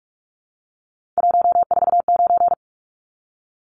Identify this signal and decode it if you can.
Morse code, heard as a keyed tone, about 44 wpm, 700 Hz. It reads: 149
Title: Numbers - 3-Digit 45wpm